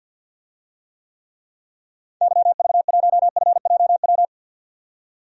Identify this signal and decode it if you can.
Morse code, heard as a keyed tone, about 46 wpm, 700 Hz. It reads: YV1FJW
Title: DXCC-100 Call Signs - 6 Characters - 50wpm